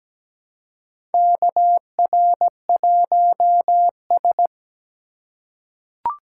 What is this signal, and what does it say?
Morse code, heard as a keyed tone, about 17 wpm, 700 Hz. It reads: KR1S
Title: USA Call Signs - 2x Character Spacing - 17wpm